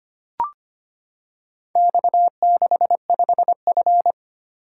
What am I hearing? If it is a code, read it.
Morse code, heard as a keyed tone, about 25 wpm, 700 Hz. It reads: X65F
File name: practice / 25wpm / Four Letter-Number Combinations - 25wpm